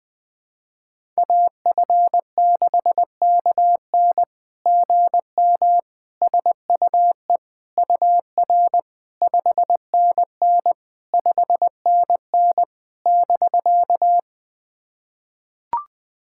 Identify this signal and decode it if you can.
Morse code, heard as a keyed tone, about 20 wpm, 690 Hz. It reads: AF6KN GM SUE UR 5NN 5NN <BK>
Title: SOTA QSO Practice 20wpm